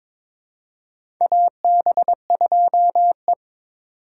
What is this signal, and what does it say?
Morse code, heard as a keyed tone, about 22 wpm, 700 Hz. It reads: AB2E
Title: USA Call Signs - 2x Character Spacing - 22wpm